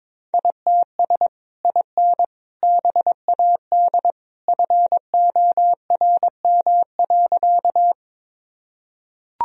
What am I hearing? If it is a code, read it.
Morse code, heard as a keyed tone, about 22 wpm, 700 Hz. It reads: ITS IN BAD FORM.
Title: Sentences from Top 1000 Words - Review Part 1 - 22wpm